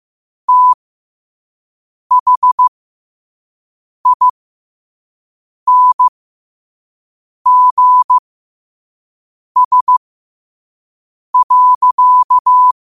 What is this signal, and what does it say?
Morse code, heard as a keyed tone, about 15 wpm, spaced as for 5 wpm, 1000 Hz. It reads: THINGS.
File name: qotdfiles / 5wpm